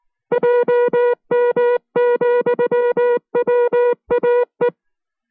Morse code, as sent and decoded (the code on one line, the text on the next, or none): .--- -- --..-- .-- .- .
JM,WAE